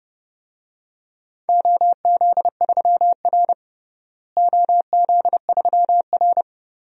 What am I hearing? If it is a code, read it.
Morse code, heard as a keyed tone, about 30 wpm, 700 Hz. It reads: OZ3R OZ3R
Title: DXCC-100 Call Signs - 4 Characters - Repeated 2x - 30wpm